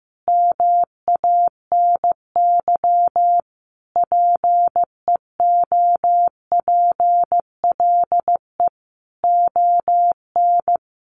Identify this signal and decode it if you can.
Morse code, heard as a keyed tone, about 15 wpm, 700 Hz. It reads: MANY PEOPLE ON
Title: Sentences from Top 1100 Words 15wpm